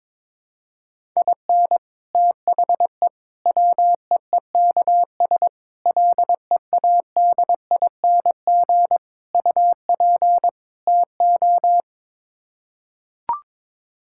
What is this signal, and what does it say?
Morse code, heard as a keyed tone, about 22 wpm, 700 Hz. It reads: IN THE WEEKS LEADING UP TO
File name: practice / 22wpm / Sets of 6 Words - 22wpm